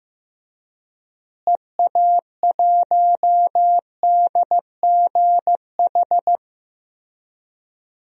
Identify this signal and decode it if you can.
Morse code, heard as a keyed tone, about 15 wpm, 700 Hz. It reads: EA1DGH
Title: DXCC-100 Call Signs - 6 Characters - Repeated 3x - 15wpm